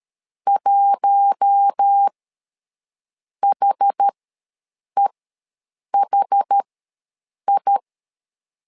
Morse code, heard as a keyed tone, about 13 wpm, spaced as for 7 wpm, 800 Hz. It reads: 1HEHI
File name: LEZIONI / 4 Q1 ws 12wpm 2129